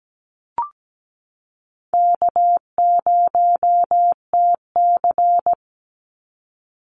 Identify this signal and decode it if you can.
Morse code, heard as a keyed tone, about 17 wpm, 700 Hz. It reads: K0TC
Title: Call Signs Repeated 3x - 17wpm